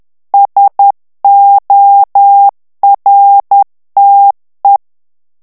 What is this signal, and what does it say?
Morse code, heard as a keyed tone, about 10 wpm, 800 Hz. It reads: SORTE
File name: CW / FICHIERS-MP3 / FRANCAIS TEXTE CLAIR 2 VITESSE 10 MOTS MINUTES